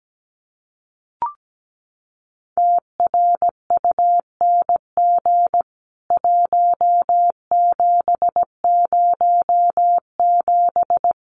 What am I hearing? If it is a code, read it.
Morse code, heard as a keyed tone, about 17 wpm, 700 Hz. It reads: TRUNG 1707
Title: CWT Exchange 17wpm